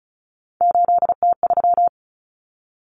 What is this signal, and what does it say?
Morse code, heard as a keyed tone, about 34 wpm, 700 Hz. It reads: 8T3
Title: Common Cut Numbers - 3-Digit 35wpm